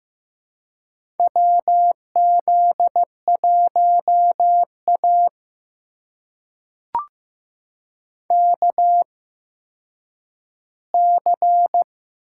Morse code, heard as a keyed tone, about 15 wpm, spaced as for 9 wpm, 700 Hz. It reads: WZ1A KC
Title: USA Call Signs - 4 Characters - 8x Character Spacing - 15wpm